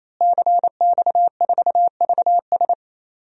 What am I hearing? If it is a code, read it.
Morse code, heard as a keyed tone, about 28 wpm, 700 Hz. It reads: CX4VS
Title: DXCC-100 Call Signs - 5 Characters - Repeated 3x - 28wpm